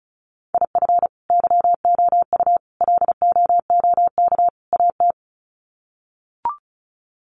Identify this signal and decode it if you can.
Morse code, heard as a keyed tone, about 34 wpm, 700 Hz. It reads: IF YOU LOOK AT
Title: Sets of 4 Words 35wpm